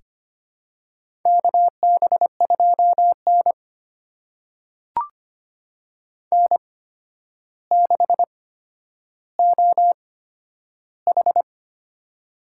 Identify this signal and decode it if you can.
Morse code, heard as a keyed tone, about 25 wpm, spaced as for 11 wpm, 700 Hz. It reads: KB2N N6OH
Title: USA Call Signs - 4 Characters - 8x Character Spacing - 25wpm